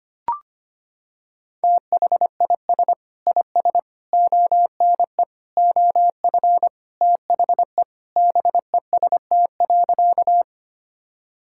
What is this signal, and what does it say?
Morse code, heard as a keyed tone, about 25 wpm, 700 Hz. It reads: THIS IS ONE OF THE BEST.
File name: practice / 25wpm / Sentences with 4-Word Phrases - 25wpm